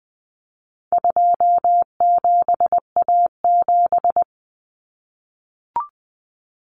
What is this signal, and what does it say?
Morse code, heard as a keyed tone, about 20 wpm, 700 Hz. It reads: 27A7
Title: Common Cut Numbers - 4-Digit 20wpm